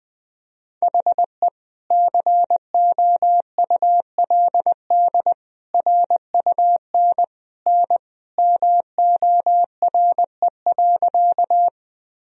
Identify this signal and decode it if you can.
Morse code, heard as a keyed tone, about 20 wpm, 700 Hz. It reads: HE COULD RUN N MORE.
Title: Sentences from Top 300 Words and Abbreviations 20wpm